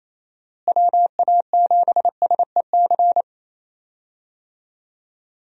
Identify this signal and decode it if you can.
Morse code, heard as a keyed tone, about 27 wpm, 700 Hz. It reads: WA7SEC
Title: USA Call Signs - 6 Characters - Repeated 3x - 28wpm